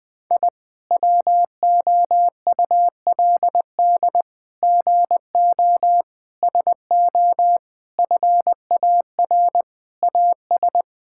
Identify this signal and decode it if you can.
Morse code, heard as a keyed tone, about 21 wpm, 700 Hz. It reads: I WOULD GO SO FAR AS
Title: Sentences with 8-Word Phrases - Spoken Beginning - 20wpm